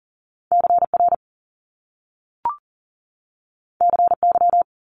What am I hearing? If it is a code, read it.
Morse code, heard as a keyed tone, about 38 wpm, 700 Hz. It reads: CR CY
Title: Top Two Letter Combinations in Words 40wpm